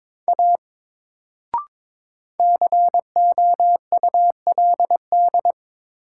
Morse code, heard as a keyed tone, about 22 wpm, 700 Hz. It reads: A COULD
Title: Sentences from Top 100 Words - New Word Drill - 22wpm